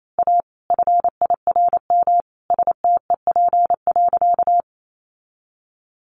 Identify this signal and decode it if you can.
Morse code, heard as a keyed tone, about 28 wpm, 700 Hz. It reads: A FIRM STEP.
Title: Sentences from Top 1400 Words 28wpm